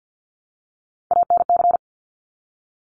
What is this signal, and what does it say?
Morse code, heard as a keyed tone, about 50 wpm, 700 Hz. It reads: ANC
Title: Top Three Letter Combinations in Words 50wpm